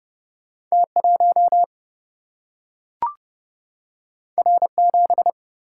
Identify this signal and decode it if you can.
Morse code, heard as a keyed tone, about 30 wpm, 700 Hz. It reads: T1 R7
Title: Single-Letter Single-Number 30wpm